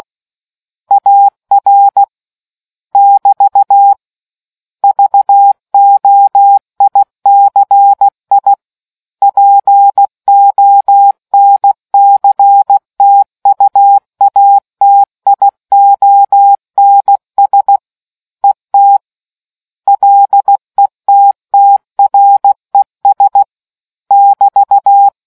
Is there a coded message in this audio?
Yes: Morse code from 0.9 to 25.1 s, heard as a keyed tone, about 16 wpm, 800 Hz. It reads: AR <BT> VOICI PONCTUATIONS ET LETTRES <BT>